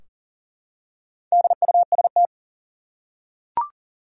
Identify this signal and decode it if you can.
Morse code, heard as a keyed tone, about 40 wpm, 700 Hz. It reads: DUST
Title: Four Letter Words - 40wpm